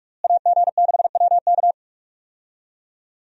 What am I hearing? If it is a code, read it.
Morse code, heard as a keyed tone, about 41 wpm, 700 Hz. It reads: AG6WK
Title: USA Call Signs - 5 Characters - Repeated 3x - 45wpm